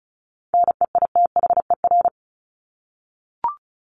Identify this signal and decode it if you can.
Morse code, heard as a keyed tone, about 34 wpm, 700 Hz. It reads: NEITHER
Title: Top 1501-1750 Words 35wpm